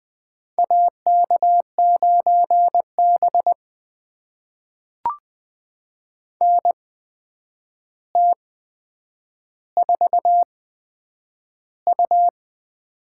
Morse code, heard as a keyed tone, about 20 wpm, spaced as for 9 wpm, 700 Hz. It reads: AK9B NT4U
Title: USA Call Signs - 4 Characters - 8x Character Spacing - 20wpm